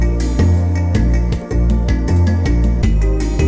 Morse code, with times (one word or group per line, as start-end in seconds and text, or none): none